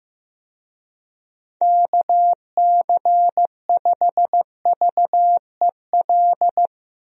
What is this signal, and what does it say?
Morse code, heard as a keyed tone, about 15 wpm, 700 Hz. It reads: KC5VEL
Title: USA Call Signs - 6 Characters - Repeated 3x - 15wpm